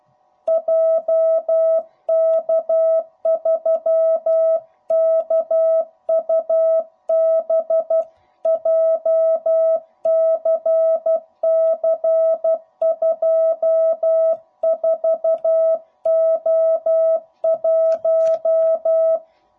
Morse code, heard as a keyed tone, about 12 wpm, 660 Hz. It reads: JK3KUB JCC24O1
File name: jh8cbh / 2021cw / cw05b